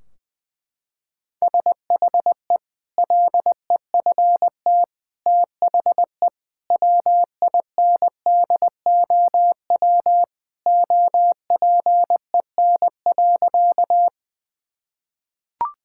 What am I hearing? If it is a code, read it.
Morse code, heard as a keyed tone, about 20 wpm, 700 Hz. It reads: SHE LEFT THE WINDOW OPEN.
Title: Sentences from Top 1600 Words 20wpm